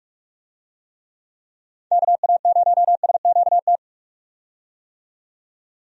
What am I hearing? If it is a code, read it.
Morse code, heard as a keyed tone, about 43 wpm, 700 Hz. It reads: KA0SQT
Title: USA Call Signs - 6 Characters - 45wpm